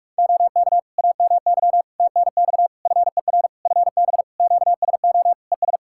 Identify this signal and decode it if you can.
Morse code, heard as a keyed tone, about 46 wpm, 700 Hz. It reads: OK AMY TNX FER FB QSO ES
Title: Practice QSO - Exchange 4 - Ending - 45wpm